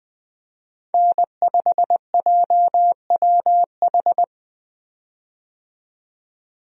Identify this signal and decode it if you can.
Morse code, heard as a keyed tone, about 20 wpm, 700 Hz. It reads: N5JWH
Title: USA Call Signs - 5 Characters - 20wpm